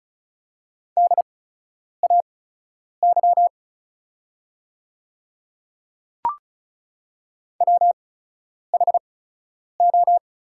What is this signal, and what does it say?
Morse code, heard as a keyed tone, about 36 wpm, spaced as for 10 wpm, 690 Hz. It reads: DAY WHO
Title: 8x Character Spacing - Top 100 Words - 35wpm